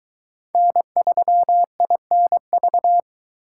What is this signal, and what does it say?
Morse code, heard as a keyed tone, about 23 wpm, 700 Hz. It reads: N3INV